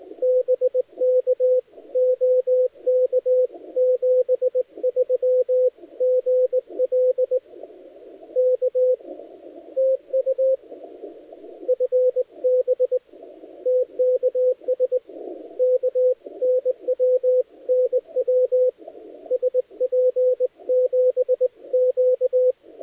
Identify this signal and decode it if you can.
Morse code, heard as a keyed tone, about 18 wpm, 500 Hz, with radio noise. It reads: BK OK 73 GL K TU FB TKS K NW NW SP7Q
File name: dlazaawansowanych / nagraniazpasma / titawaNETcontrol